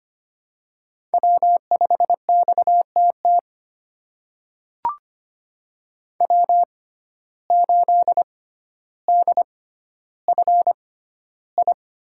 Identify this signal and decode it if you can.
Morse code, heard as a keyed tone, about 25 wpm, spaced as for 13 wpm, 700 Hz. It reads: W5XTT W8DFI
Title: USA Call Signs - 5 Characters - 6x Character Spacing - 25wpm